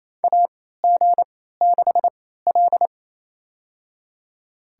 Morse code, heard as a keyed tone, about 28 wpm, spaced as for 16 wpm, 700 Hz. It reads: AG6L
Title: USA Call Signs - 3x Character Spacing - 28wpm